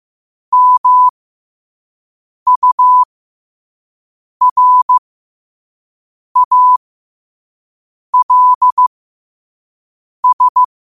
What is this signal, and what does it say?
Morse code, heard as a keyed tone, about 15 wpm, spaced as for 5 wpm, 1000 Hz. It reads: MURALS
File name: qotdfiles / 5wpm